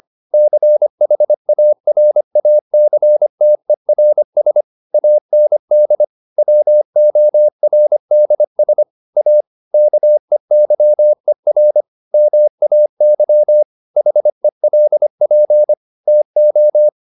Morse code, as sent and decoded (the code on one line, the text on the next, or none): -.-. .... .- .-. .- -.-. - . .-. ... / .- -. -.. / .-- --- .-. -.. ... / .- / -.- . -.-- . .-. / -- .- -.-- / .... .-.. .--. / - ---
CHARACTERS AND WORDS A KEYER MAY HELP TO